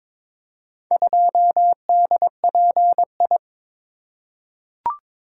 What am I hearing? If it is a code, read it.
Morse code, heard as a keyed tone, about 22 wpm, 700 Hz. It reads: 2DPI
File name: practice / 22wpm / Four Letter-Number Combinations - 22wpm